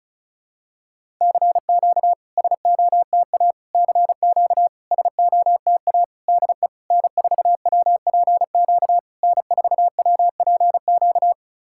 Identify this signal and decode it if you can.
Morse code, heard as a keyed tone, about 36 wpm, 690 Hz. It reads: CQ SOTA CQ SOTA DE N4WPQ N4WPQ
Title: SOTA QSO Practice 35wpm